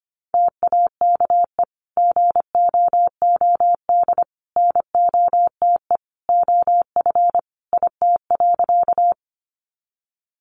Morse code, heard as a keyed tone, about 25 wpm, 700 Hz. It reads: TAKE GOOD NOTE OF IT.
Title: Sentences from Top 500 Words 25wpm